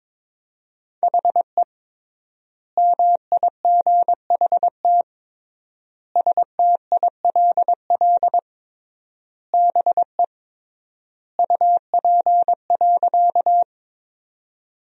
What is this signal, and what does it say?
Morse code, heard as a keyed tone, about 23 wpm, 700 Hz. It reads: HE MIGHT STILL BE UP.
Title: 3x Word-Spacing - Sentences from Top 300 Words - 22wpm